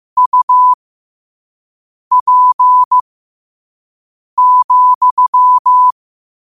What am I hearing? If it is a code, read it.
Morse code, heard as a keyed tone, about 15 wpm, spaced as for 5 wpm, 1000 Hz. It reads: UP,
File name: qotdfiles / 5wpm